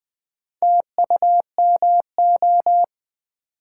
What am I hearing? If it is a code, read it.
Morse code, heard as a keyed tone, about 20 wpm, 700 Hz. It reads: TUMO
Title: Build A Word - 1 Letter At A Time - 5-Letter Words - 20wpm